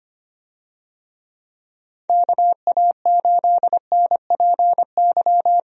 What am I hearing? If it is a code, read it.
Morse code, heard as a keyed tone, about 25 wpm, 700 Hz. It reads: KA8NPY
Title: USA Call Signs - 6 Characters - 25wpm